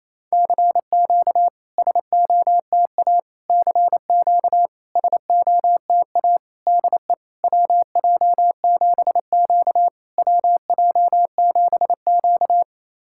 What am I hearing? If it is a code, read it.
Morse code, heard as a keyed tone, about 27 wpm, 690 Hz. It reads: CQ SOTA CQ SOTA DE WJ7Q WJ7Q
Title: SOTA QSO Practice 28wpm